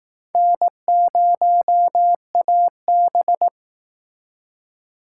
Morse code, heard as a keyed tone, about 19 wpm, 700 Hz. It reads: N0AB